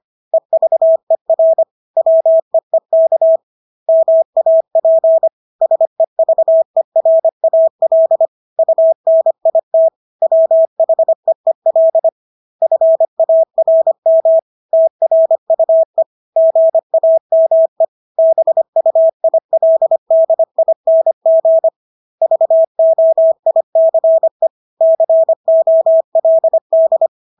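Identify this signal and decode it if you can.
Morse code, heard as a keyed tone, about 25 wpm, 650 Hz. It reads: EVER WEEK MAP SEVERAL UNIT WHEEL FARM TRUE GAME BUILDING VOICE COLD
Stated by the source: Morse practice of English words at 25 wpm from CW med Gnister.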